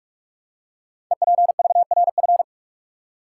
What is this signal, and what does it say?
Morse code, heard as a keyed tone, about 44 wpm, 690 Hz. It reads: EPVRF
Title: Code Groups - 5 Letters - 45wpm